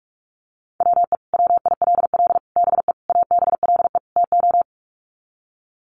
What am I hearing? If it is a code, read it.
Morse code, heard as a keyed tone, about 44 wpm, 700 Hz. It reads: WE WILL BE ABLE TO
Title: Sets of 5 Words Encore 45wpm